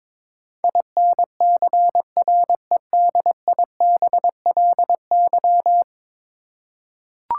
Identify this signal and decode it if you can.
Morse code, heard as a keyed tone, about 22 wpm, 700 Hz. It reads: INCREDIBLY
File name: practice / 22wpm / 10 Letter Words - First 3 Letters Given - 22wpm